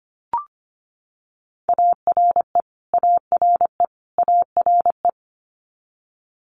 Morse code, heard as a keyed tone, about 25 wpm, 700 Hz. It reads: ARE ARE ARE
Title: Multi-letter Phonemes Repeated 3x 25wpm